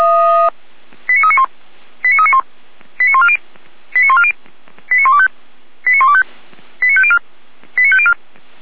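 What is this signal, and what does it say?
Five-tone selective call (ZVEI), heard as quick runs of five tones, tones of about 70 ms. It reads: 89391 89391 89140 89140 89125 89125 89504 89504
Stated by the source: Five-tone ZVEI selective call received on SDR, from audio.f85360.feuerwehr-orlamuende.org (Feuerwehr Orlamünde, fire brigade).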